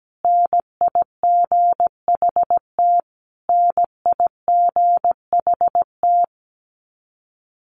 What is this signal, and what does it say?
Morse code, heard as a keyed tone, about 17 wpm, 700 Hz. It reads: NIGHT NIGHT
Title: Top 500 Words Repeated 17wpm